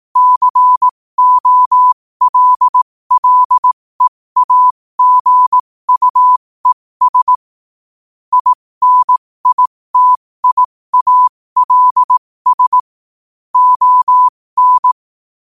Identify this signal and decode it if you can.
Morse code, heard as a keyed tone, about 18 wpm, spaced as for 15 wpm, 1000 Hz. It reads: COLLEAGUES INITIALS ON
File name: qotdfiles / 15wpm